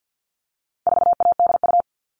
Morse code, heard as a keyed tone, about 50 wpm, 700 Hz. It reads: 4ADU
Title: All Cut Numbers - 4-Digit 50wpm